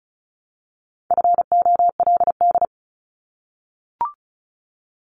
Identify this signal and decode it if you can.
Morse code, heard as a keyed tone, about 36 wpm, 700 Hz. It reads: FOLD
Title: Multi-letter Phonemes Repeated 3x 35wpm